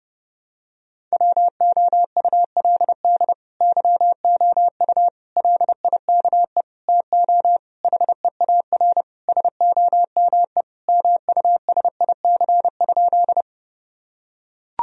Morse code, heard as a keyed tone, about 30 wpm, 700 Hz. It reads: WOULD YOU LIKE TO HEAR SOME MUSIC?
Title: Sentences from Top 1000 Words - Review Part 1 - 30wpm